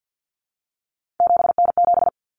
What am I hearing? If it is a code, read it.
Morse code, heard as a keyed tone, about 50 wpm, 700 Hz. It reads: 7N7
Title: Common Cut Numbers - 3-Digit 50wpm